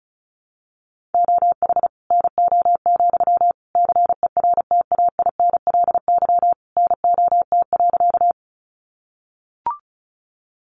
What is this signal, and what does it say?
Morse code, heard as a keyed tone, about 34 wpm, 700 Hz. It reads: OH NO, CERTAINLY NOT.
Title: Sentences from Top 900 Words 35wpm